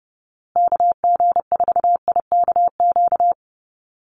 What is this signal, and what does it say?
Morse code, heard as a keyed tone, about 30 wpm, 700 Hz. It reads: KG4IKQ
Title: Call Signs Repeated 2x - 30wpm